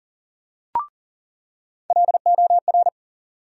Morse code, heard as a keyed tone, about 39 wpm, 700 Hz. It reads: LOR
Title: USA Call Sign Postfixes - 3 Letters 40wpm